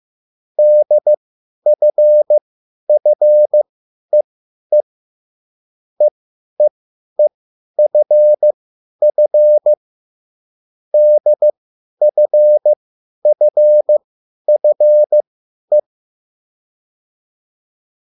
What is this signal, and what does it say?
Morse code, heard as a keyed tone, about 15 wpm, spaced as for 10 wpm, 600 Hz. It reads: DFFEE EEEFF DFFFE